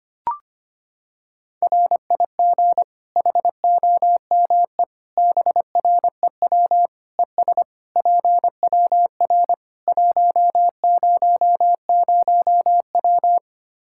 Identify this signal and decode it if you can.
Morse code, heard as a keyed tone, about 25 wpm, 700 Hz. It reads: RIG HOME BREW ES PWR 100W
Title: Practice QSO - Exchange 2 - Info - 25wpm